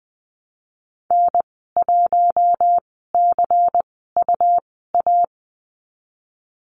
Morse code, heard as a keyed tone, about 20 wpm, spaced as for 14 wpm, 700 Hz. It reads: N1CUA
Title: 128 Lesson 21g - Number 6 - Call Signs 20wpm